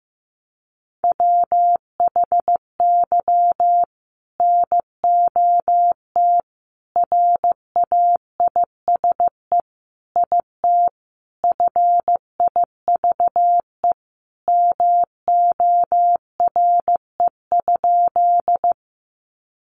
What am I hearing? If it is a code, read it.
Morse code, heard as a keyed tone, about 15 wpm, 700 Hz. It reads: WHY NOT RAISE IT FIVE MORE?
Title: Sentences from Top 1400 Words 15wpm